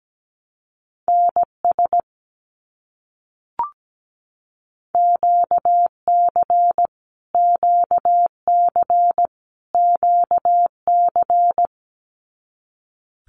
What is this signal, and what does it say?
Morse code, heard as a keyed tone, about 17 wpm, 700 Hz. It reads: NS QC QC QC
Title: Canadian Provinces and Territories - Repeated 3x - 17wpm